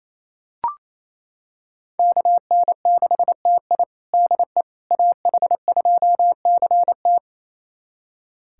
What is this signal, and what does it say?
Morse code, heard as a keyed tone, about 28 wpm, 700 Hz. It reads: KN6TI DE AH2CT
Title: USA Callsign DE Callsign - 5 Characters - 28wpm